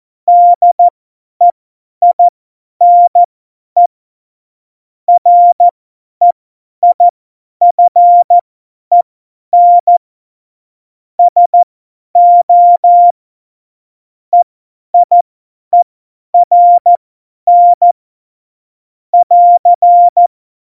Morse code, heard as a keyed tone, about 14 wpm, spaced as for 10 wpm, 700 Hz. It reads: DEINE REIFEN SO EIERN <AR>
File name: Morsekurs / Witze / Witz 21 April 2020, 50 F70